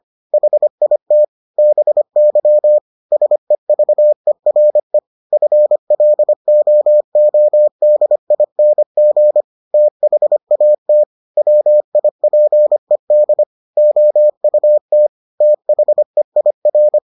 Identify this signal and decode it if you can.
Morse code, heard as a keyed tone, about 25 wpm, 600 Hz. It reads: HIT BY SEVERE FLOODING THAT WIPED OUT THEIR